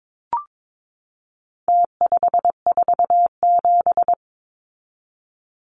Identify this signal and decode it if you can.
Morse code, heard as a keyed tone, about 22 wpm, 700 Hz. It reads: T547
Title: Numbers - Contest Style - 4-Digit 22wpm